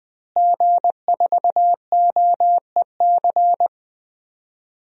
Morse code, heard as a keyed tone, about 20 wpm, 700 Hz. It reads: G4OEC
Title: DXCC-100 Call Signs - 5 Characters - Repeated 2x - 20wpm